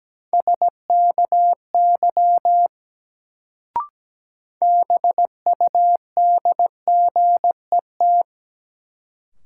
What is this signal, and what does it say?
Morse code, heard as a keyed tone, about 17 wpm, 700 Hz. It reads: SKY BUDGET
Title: Sentences from Top 1200 Words - New Word Drill Alt - 17wpm